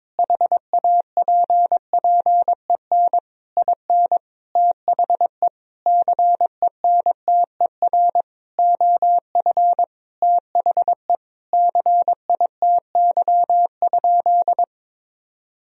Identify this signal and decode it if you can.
Morse code, heard as a keyed tone, about 21 wpm, 700 Hz. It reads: HAPPEN IN THE CENTER OF THE CITY?
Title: Sentences from Top 1000 Words - Review Part 2 - 22wpm